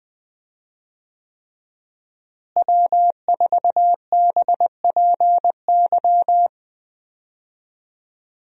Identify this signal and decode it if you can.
Morse code, heard as a keyed tone, about 21 wpm, 700 Hz. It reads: W4BPY
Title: USA Call Signs - 5 Characters - Repeated 3x - 20wpm